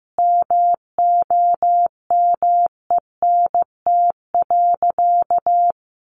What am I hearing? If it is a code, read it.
Morse code, heard as a keyed tone, about 15 wpm, 700 Hz. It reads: MOMENT.
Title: Sentences from Top 1000 Words 15wpm